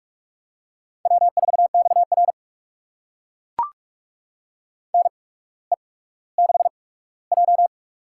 Morse code, heard as a keyed tone, about 44 wpm, spaced as for 19 wpm, 700 Hz. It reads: W4XR NE6J
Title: USA Call Signs - 4 Characters - 8x Character Spacing - 45wpm